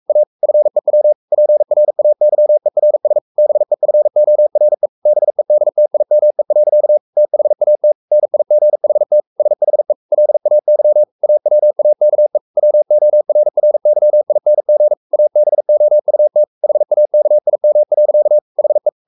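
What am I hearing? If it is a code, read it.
Morse code, heard as a keyed tone, about 44 wpm, 600 Hz. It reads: A FEW PRAYERS BEFORE BEDTIME. THAT NIGHT SHE LAY AWAKE WORRYING ABOUT HAKIM. HE
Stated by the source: Morse code at 43 wpm sending Ken Follett's novel Never in English, from CW med Gnister.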